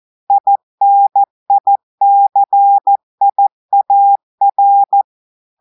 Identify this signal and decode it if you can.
Morse code, heard as a keyed tone, about 14 wpm, 800 Hz. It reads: INICIAR
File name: audios / portugais / portugais texte en clair 2 vitesse 14 mots minutes